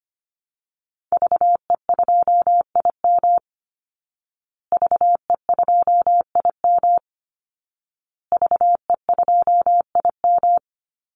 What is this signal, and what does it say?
Morse code, heard as a keyed tone, about 25 wpm, 700 Hz. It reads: VE2IM VE2IM VE2IM
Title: Call Signs Repeated 2x - 25wpm